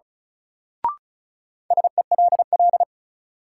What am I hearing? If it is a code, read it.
Morse code, heard as a keyed tone, about 36 wpm, 700 Hz. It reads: SELL
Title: Sentences from Top 1100 Words - New Word Drill Alt - 35wpm